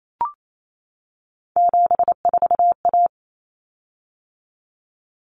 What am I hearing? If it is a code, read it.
Morse code, heard as a keyed tone, about 28 wpm, 700 Hz. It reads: 74A
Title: Common Cut Numbers - 3-Digit 28wpm